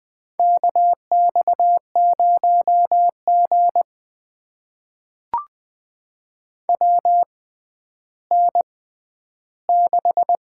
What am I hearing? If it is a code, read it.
Morse code, heard as a keyed tone, about 20 wpm, spaced as for 11 wpm, 700 Hz. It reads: KX0G WN6
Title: USA Call Signs - 4 Characters - 6x Character Spacing - 20wpm